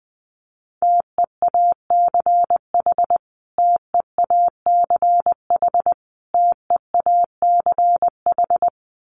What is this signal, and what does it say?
Morse code, heard as a keyed tone, about 20 wpm, 700 Hz. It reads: TEACH TEACH TEACH
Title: Top 500 Words Repeated 20wpm